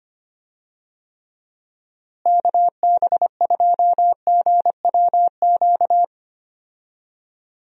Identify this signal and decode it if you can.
Morse code, heard as a keyed tone, about 25 wpm, 700 Hz. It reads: KB2GWQ
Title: USA Call Signs - 6 Characters - Repeated 3x - 25wpm